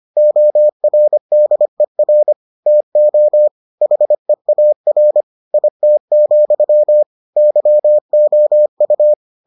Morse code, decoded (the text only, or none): ORDER TO HEAR IT, YOU